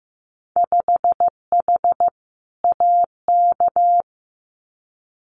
Morse code, heard as a keyed tone, about 15 wpm, 700 Hz. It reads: 5H AK
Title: Winter Field Day Exchanges Encore 15wpm